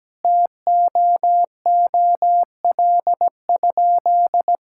Morse code, heard as a keyed tone, about 17 wpm, 700 Hz. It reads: TOOL?
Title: Sentences from Top 1000 Words - Review Part 6 - 17wpm